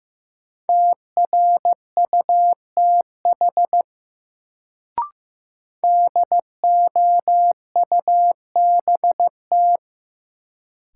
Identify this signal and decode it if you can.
Morse code, heard as a keyed tone, about 15 wpm, 700 Hz. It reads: TRUTH DOUBT
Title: Sets of 5 Words - New Word Drill - 15wpm